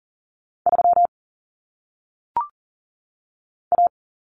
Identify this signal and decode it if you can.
Morse code, heard as a keyed tone, about 40 wpm, 700 Hz. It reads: 3 A
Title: Common Cut Numbers - 1-Digit 40wpm